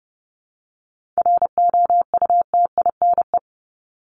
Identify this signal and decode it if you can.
Morse code, heard as a keyed tone, about 30 wpm, 700 Hz. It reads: ROUTINE
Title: Top 1501-1750 Words 30wpm